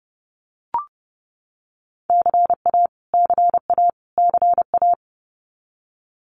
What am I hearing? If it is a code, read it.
Morse code, heard as a keyed tone, about 30 wpm, 700 Hz. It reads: CA CA CA
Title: US States Repeated 3x 30wpm